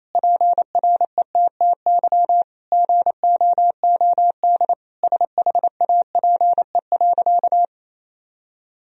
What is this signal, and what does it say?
Morse code, heard as a keyed tone, about 27 wpm, 690 Hz. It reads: PRETTY GOOD SHAPE.
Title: Sentences from Top 1000 Words - Review Part 3 - 28wpm